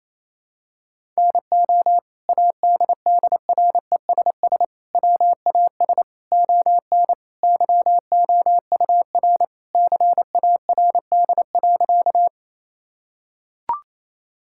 Morse code, heard as a keyed tone, about 28 wpm, 700 Hz. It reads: NO ADDRESS WAS ON YOUR CARD.
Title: Sentences from Top 1000 Words - Review Part 6 - 28wpm